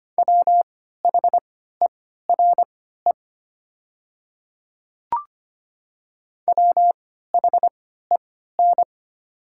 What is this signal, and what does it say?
Morse code, heard as a keyed tone, about 25 wpm, spaced as for 14 wpm, 700 Hz. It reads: WHERE WHEN
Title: 3x Character Spacing - Top 200 Words - 25wpm